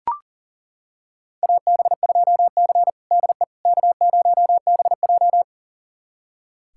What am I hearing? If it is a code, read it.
Morse code, heard as a keyed tone, about 40 wpm, 700 Hz. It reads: AB2C DE K0BJ
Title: USA Callsign DE Callsign - 4 Characters - 40wpm